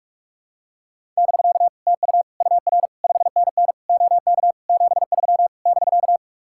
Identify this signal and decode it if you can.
Morse code, heard as a keyed tone, about 49 wpm, 700 Hz. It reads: <BK> TU UR 5NN OK 73 <BK>